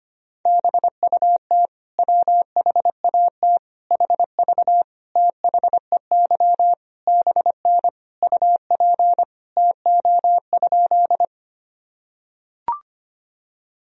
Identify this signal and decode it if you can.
Morse code, heard as a keyed tone, about 25 wpm, 700 Hz. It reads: BUT WHAT HV THEY BN UP TO?
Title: Sentences from Top 100 Words and Abbreviations 25wpm